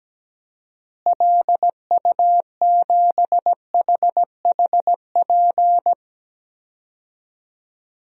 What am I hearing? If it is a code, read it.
Morse code, heard as a keyed tone, about 17 wpm, 700 Hz. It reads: LU7HHP
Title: DXCC-100 Call Signs - 6 Characters - Repeated 2x - 17wpm